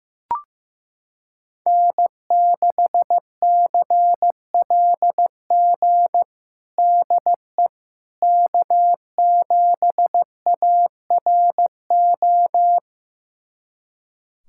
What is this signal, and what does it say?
Morse code, heard as a keyed tone, about 15 wpm, 700 Hz. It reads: N6CLG DE K7ARO
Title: USA Callsign DE Callsign - 5 Characters - 15wpm